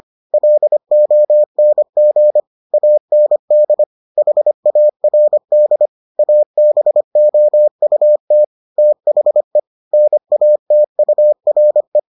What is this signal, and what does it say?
Morse code, heard as a keyed tone, about 25 wpm, 600 Hz. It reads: LONG AND HARD ABOUT THE NATURE